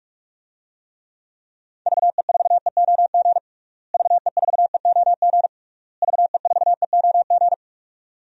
Morse code, heard as a keyed tone, about 46 wpm, 700 Hz. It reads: VE4EOG VE4EOG VE4EOG
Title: DXCC-100 Call Signs - 6 Characters - Repeated 3x - 45wpm